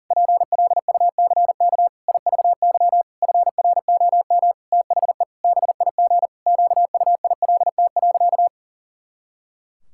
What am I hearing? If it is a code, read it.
Morse code, heard as a keyed tone, about 40 wpm, 700 Hz. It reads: PLUCK IVY FROM THE BIG QUILT.
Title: Pangram Challenge v2 - 40wpm